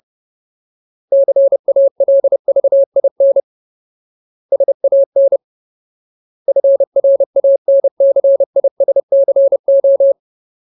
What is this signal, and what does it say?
Morse code, heard as a keyed tone, about 30 wpm, 550 Hz. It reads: CALVIN SAN FRANCISCO
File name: Ultimate Word List / 30WPM Ultimate